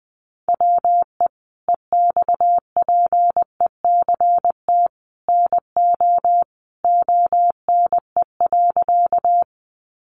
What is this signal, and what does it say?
Morse code, heard as a keyed tone, about 20 wpm, 700 Hz. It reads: WE EXPECT NO ONE.
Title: Sentences from Top 1100 Words 20wpm